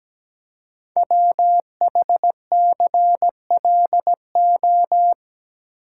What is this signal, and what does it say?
Morse code, heard as a keyed tone, about 17 wpm, 700 Hz. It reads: WHCLO
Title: Code Groups - 5 Letter:Numbers - 17wpm